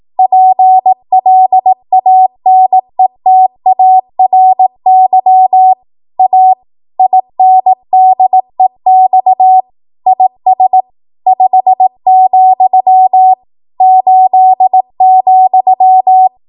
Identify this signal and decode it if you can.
Morse code, heard as a keyed tone, about 18 wpm, 750 Hz. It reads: PLANETARY A INDEX IS 5, 8,
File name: Morse / 18WPM